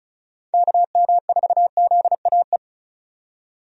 Morse code, heard as a keyed tone, about 36 wpm, 700 Hz. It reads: KM4ZAE
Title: USA Call Signs - 6 Characters - 35wpm